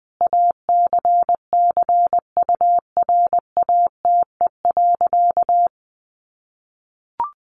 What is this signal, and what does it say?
Morse code, heard as a keyed tone, about 20 wpm, 700 Hz. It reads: ACCURATE.
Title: Sentences from Top 1200 Words 20wpm